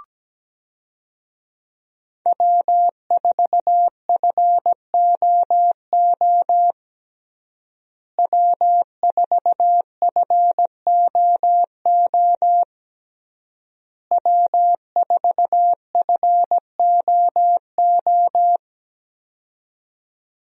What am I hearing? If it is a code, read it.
Morse code, heard as a keyed tone, about 17 wpm, 700 Hz. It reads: W4FOO W4FOO W4FOO
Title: USA Call Signs - 5 Characters - Repeated 3x - 17wpm